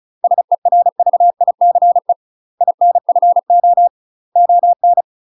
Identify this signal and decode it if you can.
Morse code, heard as a keyed tone, about 34 wpm, 700 Hz. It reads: SERVICE INFO ON
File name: UK MASTER QSOs TRAINING/35WPM